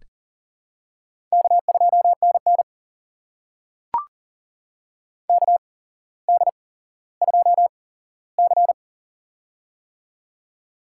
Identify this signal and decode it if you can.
Morse code, heard as a keyed tone, about 40 wpm, spaced as for 17 wpm, 700 Hz. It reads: K2NN KD2C
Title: USA Call Signs - 4 Characters - 8x Character Spacing - 40wpm